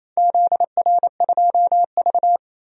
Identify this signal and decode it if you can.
Morse code, heard as a keyed tone, about 28 wpm, 700 Hz. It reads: ZR2V
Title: DXCC-100 Call Signs - 4 Characters - 28wpm